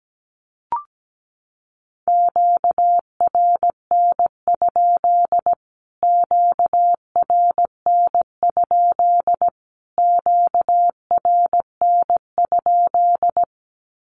Morse code, heard as a keyed tone, about 17 wpm, 700 Hz. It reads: QRN? QRN? QRN?
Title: Q-Codes Repeated 3x 17wpm